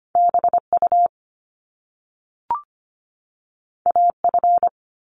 Morse code, heard as a keyed tone, about 25 wpm, 700 Hz. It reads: BU AF
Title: Top Two Letter Combinations in Words 25wpm